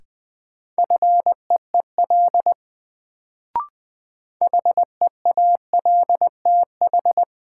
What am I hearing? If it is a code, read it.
Morse code, heard as a keyed tone, about 21 wpm, 700 Hz. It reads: FEEL HEALTH
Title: Sets of 5 Words - New Word Drill - 20wpm